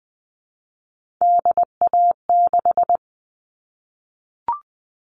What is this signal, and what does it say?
Morse code, heard as a keyed tone, about 20 wpm, 700 Hz. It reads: DA6
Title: All Cut Numbers - 3-Digit 20wpm